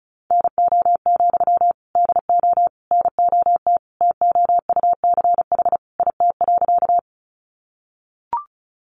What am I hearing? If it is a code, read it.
Morse code, heard as a keyed tone, about 34 wpm, 700 Hz. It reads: NO, DO NOT TOUCH IT.
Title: Sentences from Top 1200 Words 35wpm